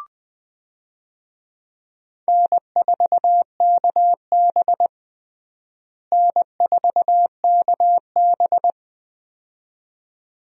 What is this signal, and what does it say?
Morse code, heard as a keyed tone, about 20 wpm, 700 Hz. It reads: N4KB N4KB
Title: USA Call Signs - 4 Characters - Repeated 2x - 20wpm